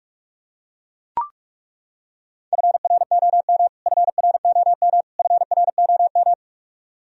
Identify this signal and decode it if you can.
Morse code, heard as a keyed tone, about 42 wpm, 700 Hz. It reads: FROM FROM FROM
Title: Top 100 Words Repeated 45wpm